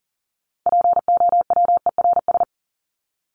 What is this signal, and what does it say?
Morse code, heard as a keyed tone, about 40 wpm, 700 Hz. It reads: POWERS
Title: ES:S Suffix - Top 100 Words with Base Words 40wpm